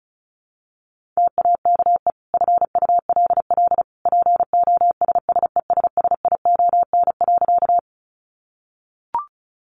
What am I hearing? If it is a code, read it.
Morse code, heard as a keyed tone, about 34 wpm, 700 Hz. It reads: TAKE FULL POSSESSION.
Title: Sentences from Top 1400 Words 35wpm